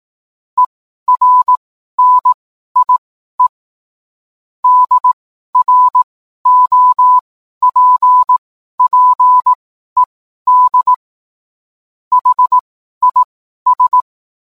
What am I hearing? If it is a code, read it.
Morse code, heard as a keyed tone, about 18 wpm, spaced as for 12 wpm, 1000 Hz. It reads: ERNIE DROPPED HIS